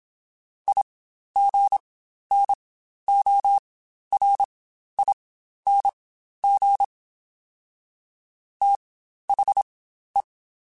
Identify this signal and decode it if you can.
Morse code, heard as a keyed tone, about 26 wpm, spaced as for 12 wpm, 800 Hz. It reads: IGNORING THE
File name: Quotes Spaced / Quotes Spaced-mp3s / IgnoringFacts sp @25